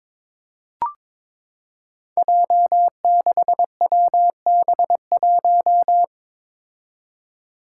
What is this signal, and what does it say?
Morse code, heard as a keyed tone, about 22 wpm, 700 Hz. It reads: J6WB1
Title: Five Challenging Letter-Number Combinations 22wpm